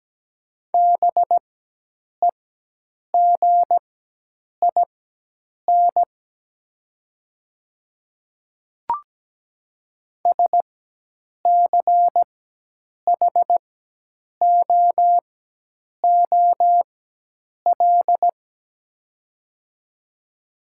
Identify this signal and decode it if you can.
Morse code, heard as a keyed tone, about 17 wpm, spaced as for 8 wpm, 700 Hz. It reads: BEGIN SCHOOL
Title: 4x Character Spacing - Top 300 Words - 17wpm